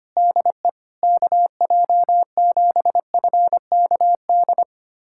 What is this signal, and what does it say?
Morse code, heard as a keyed tone, about 25 wpm, 700 Hz. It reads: DE KJ7FKD
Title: USA Callsign DE Callsign 25wpm